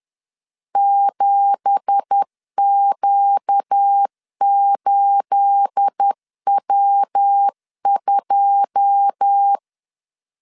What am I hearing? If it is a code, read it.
Morse code, heard as a keyed tone, about 10 wpm, 800 Hz. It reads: 7Q8W2
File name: LEZIONI / 9 2867 10wpm 2733